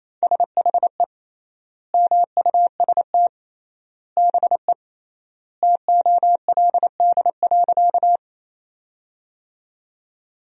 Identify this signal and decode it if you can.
Morse code, heard as a keyed tone, about 28 wpm, 700 Hz. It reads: SHE MUST BE TOLD.
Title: 3x Word-Spacing - Sentences from Top 400 Words - 28wpm